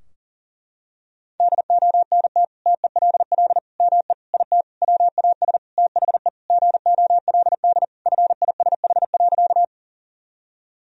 Morse code, heard as a keyed tone, about 40 wpm, 700 Hz. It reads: DONT TELL ME IT WAS THE GOLD FISH.
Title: Sentences from Top 1000 Words - Review Part 3 - 40wpm